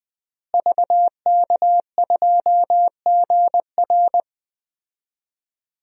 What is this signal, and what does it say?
Morse code, heard as a keyed tone, about 20 wpm, 700 Hz. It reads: VK2GR
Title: DXCC-100 Call Signs - 5 Characters - Repeated 3x - 20wpm